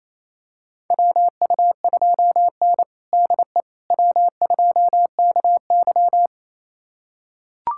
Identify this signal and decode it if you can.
Morse code, heard as a keyed tone, about 27 wpm, 700 Hz. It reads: WU2N DE W2KY
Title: USA Callsign DE Callsign - 4 Characters - 28wpm